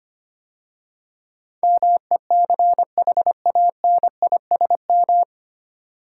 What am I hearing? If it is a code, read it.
Morse code, heard as a keyed tone, about 25 wpm, 700 Hz. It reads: MECHANISM
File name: practice / 25wpm / 9 Letter Words - First 3 Letters Given - 25wpm